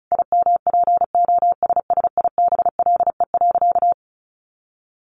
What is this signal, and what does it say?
Morse code, heard as a keyed tone, about 34 wpm, 700 Hz. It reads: IMPOSSIBLE.
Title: Sentences from Top 1300 Words 35wpm